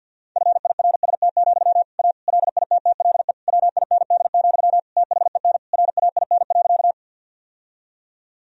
Morse code, heard as a keyed tone, about 52 wpm, 700 Hz. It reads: FIRST, A LITTLE WIND, THEN RAIN.
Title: Sentences from Top 1000 Words - Review Part 2 - 50wpm